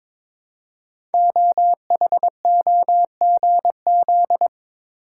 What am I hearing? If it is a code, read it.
Morse code, heard as a keyed tone, about 22 wpm, 700 Hz. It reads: OHOGZ
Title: Code Groups - 5 Letters - 22wpm